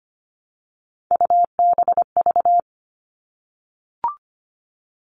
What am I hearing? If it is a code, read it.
Morse code, heard as a keyed tone, about 25 wpm, 700 Hz. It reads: UBV
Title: All Cut Numbers - 3-Digit 25wpm